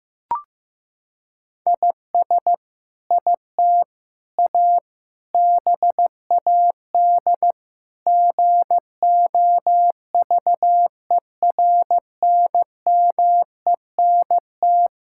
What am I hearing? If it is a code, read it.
Morse code, heard as a keyed tone, about 15 wpm, 700 Hz. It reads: IS IT A BAD GOVERNMENT
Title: Sentences from Top 1000 Words - Review Part 5 - 15wpm